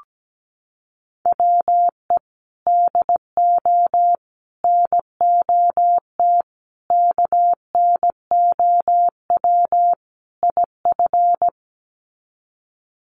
Morse code, heard as a keyed tone, about 17 wpm, 700 Hz. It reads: WE DO NOT KNOW IF
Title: Sets of 5 Words Encore2 17wpm